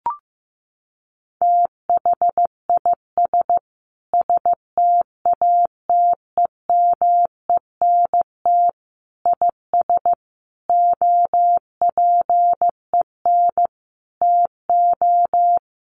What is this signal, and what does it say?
Morse code, heard as a keyed tone, about 15 wpm, 700 Hz. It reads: THIS STATEMENT IS OPEN TO
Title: Sentences from Top 1300 Words 15wpm